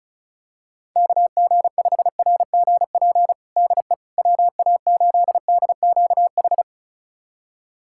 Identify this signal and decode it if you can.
Morse code, heard as a keyed tone, about 34 wpm, 700 Hz. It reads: KG5RGP DE WA8DQH
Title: USA Callsign DE Callsign 35wpm